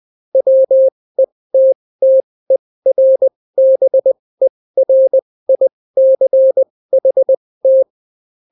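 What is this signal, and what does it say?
Morse code, heard as a keyed tone, about 20 wpm, spaced as for 16 wpm, 530 Hz. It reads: WETTERBERICHT
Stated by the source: Daily Morse weather and top stories at 16 wpm, clean without QRN.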